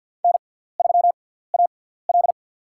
Morse code, heard as a keyed tone, about 47 wpm, spaced as for 17 wpm, 700 Hz. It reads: N3AL
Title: USA Call Signs - 4 Characters - 6x Character Spacing - 50wpm